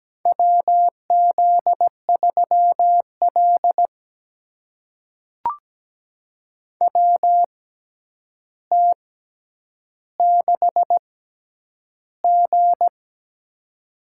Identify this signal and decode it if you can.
Morse code, heard as a keyed tone, about 17 wpm, spaced as for 9 wpm, 700 Hz. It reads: WZ3L WT6G
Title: USA Call Signs - 4 Characters - 6x Character Spacing - 17wpm